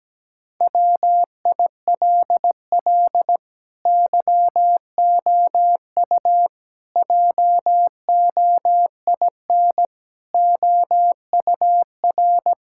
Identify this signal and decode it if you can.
Morse code, heard as a keyed tone, about 17 wpm, 700 Hz. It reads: WILL YOU JOIN OUR
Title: Sentences from Top 1600 Words 17wpm